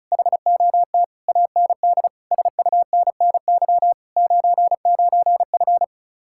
Morse code, heard as a keyed tone, about 44 wpm, 700 Hz. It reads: HOT AND SUNNY 99F
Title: Speed-Racing Common QSO Phrases Encore - 35wpm